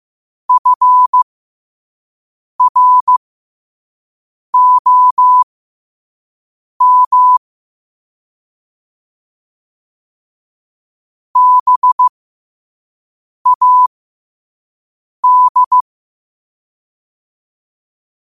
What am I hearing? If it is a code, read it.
Morse code, heard as a keyed tone, about 15 wpm, spaced as for 5 wpm, 1000 Hz. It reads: FROM BAD